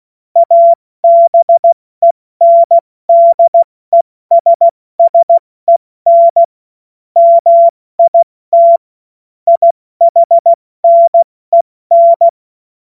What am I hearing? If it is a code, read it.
Morse code, heard as a keyed tone, about 16 wpm, 680 Hz. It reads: ABENDESSEN MIT IHNEN
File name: Morsekurs / Witze / Witz 06. Dezember 2022 70 F80